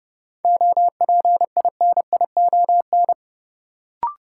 Morse code, heard as a keyed tone, about 30 wpm, 700 Hz. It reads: OPINION
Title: Sentences from Top 1200 Words - New Word Drill - 30wpm